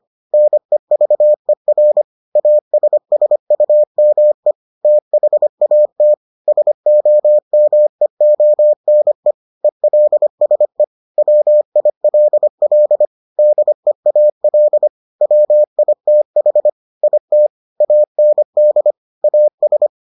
Morse code, heard as a keyed tone, about 25 wpm, 600 Hz. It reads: NEVER ASSUME THAT SOMEONE ELSE WILL DEAL WITH IT AND AS